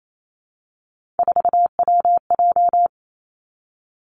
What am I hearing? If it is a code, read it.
Morse code, heard as a keyed tone, about 28 wpm, 700 Hz. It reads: 4WJ
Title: Three Challenging Letter-Number Combinations 28wpm